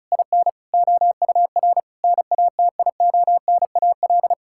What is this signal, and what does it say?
Morse code, heard as a keyed tone, about 34 wpm, 700 Hz. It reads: IN OUR NATIONAL
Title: Sentences from Top 1000 Words - Review Part 6 - 35wpm